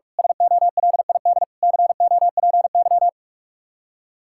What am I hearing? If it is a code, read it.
Morse code, heard as a keyed tone, about 41 wpm, 700 Hz. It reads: SOLID COPY
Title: Practice QSO - Exchange 3 - More Info - 45wpm